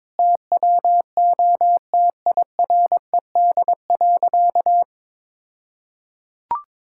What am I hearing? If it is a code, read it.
Morse code, heard as a keyed tone, about 22 wpm, 700 Hz. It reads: TWO TIRED.
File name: All-Age Appropriate Jokes - 22wpm